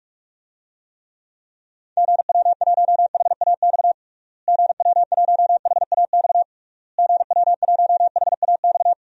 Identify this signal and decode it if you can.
Morse code, heard as a keyed tone, about 42 wpm, 700 Hz. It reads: GW1HAX GW1HAX GW1HAX
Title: DXCC-100 Call Signs - 6 Characters - Repeated 3x - 45wpm